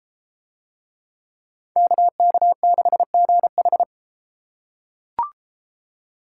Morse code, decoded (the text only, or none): KK6GH